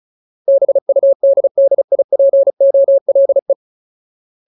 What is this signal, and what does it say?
Morse code, heard as a keyed tone, about 34 wpm, 550 Hz. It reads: BUDDIPOLE
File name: Ultimate Word List / 35WPM Ultimate